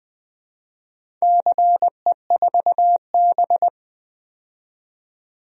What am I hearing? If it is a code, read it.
Morse code, heard as a keyed tone, about 20 wpm, 700 Hz. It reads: CE4B
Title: DXCC-100 Call Signs - 4 Characters - 20wpm